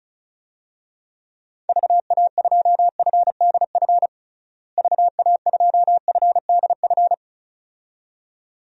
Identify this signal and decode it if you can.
Morse code, heard as a keyed tone, about 36 wpm, 700 Hz. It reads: VA2FDF VA2FDF
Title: DXCC-100 Call Signs - 6 Characters - Repeated 2x - 35wpm